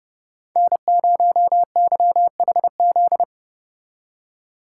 Morse code, heard as a keyed tone, about 30 wpm, 700 Hz. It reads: N0YHZ